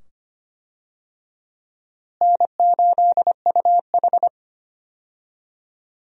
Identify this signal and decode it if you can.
Morse code, heard as a keyed tone, about 25 wpm, 700 Hz. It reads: N8UH
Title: USA Call Signs - 4 Characters - Repeated 2x - 25wpm